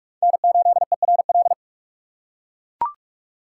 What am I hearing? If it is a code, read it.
Morse code, heard as a keyed tone, about 44 wpm, 700 Hz. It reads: N8ERL
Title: USA Call Signs - 5 Characters - 6x Character Spacing - 45wpm